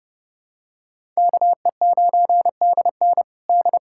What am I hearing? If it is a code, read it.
Morse code, heard as a keyed tone, about 32 wpm, 700 Hz. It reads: KE9DN D
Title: USA Callsign DE Callsign - 5 Characters - 30wpm